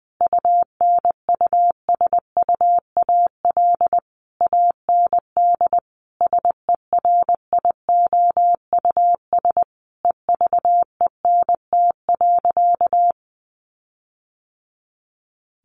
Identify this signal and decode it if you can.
Morse code, heard as a keyed tone, about 20 wpm, 700 Hz. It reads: UNUSUAL AND SERIOUS EVENT.
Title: Sentences from Top 1400 Words 20wpm